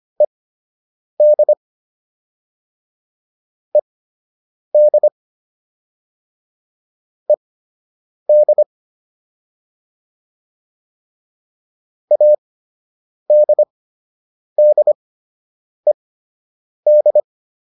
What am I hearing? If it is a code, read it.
Morse code, heard as a keyed tone, about 25 wpm, spaced as for 8 wpm, 600 Hz. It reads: ED ED ED ADDED